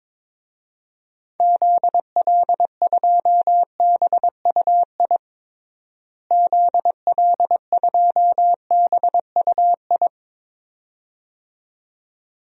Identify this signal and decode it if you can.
Morse code, heard as a keyed tone, about 22 wpm, 700 Hz. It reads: ZL2BUI ZL2BUI
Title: DXCC-100 Call Signs - 6 Characters - Repeated 2x - 22wpm